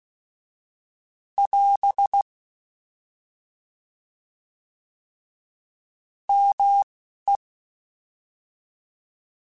Morse code, heard as a keyed tone, about 16 wpm, 800 Hz. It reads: <AS> M E